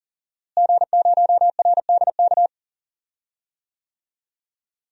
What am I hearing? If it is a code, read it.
Morse code, heard as a keyed tone, about 40 wpm, 700 Hz. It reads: G0RDK
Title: DXCC-100 Call Signs - 5 Characters - Repeated 2x - 40wpm